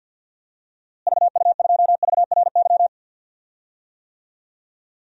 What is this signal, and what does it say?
Morse code, heard as a keyed tone, about 47 wpm, 700 Hz. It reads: VU2VRY
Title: DXCC-100 Call Signs - 6 Characters - Repeated 2x - 50wpm